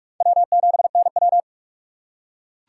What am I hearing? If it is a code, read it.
Morse code, heard as a keyed tone, about 44 wpm, 700 Hz. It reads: W7NW